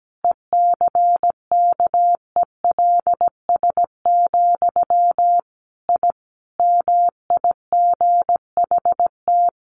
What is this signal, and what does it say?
Morse code, heard as a keyed tone, about 17 wpm, 700 Hz. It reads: ECKELS, I MIGHT
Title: A Sound Of Thunder 17wpm